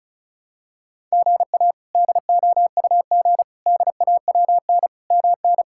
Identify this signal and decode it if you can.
Morse code, heard as a keyed tone, about 36 wpm, 700 Hz. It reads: GA DOUG DAWN MN
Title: K1USN SST QSO Practice - 35wpm